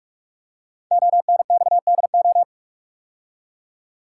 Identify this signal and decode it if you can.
Morse code, heard as a keyed tone, about 44 wpm, 700 Hz. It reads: ONXDO
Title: Code Groups - 5 Letters - 45wpm